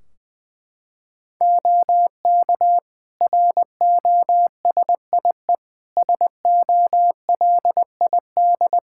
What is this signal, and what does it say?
Morse code, heard as a keyed tone, about 20 wpm, 700 Hz. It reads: OK ROSIE SOLID
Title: Practice QSO - Exchange 3 - More Info - 20wpm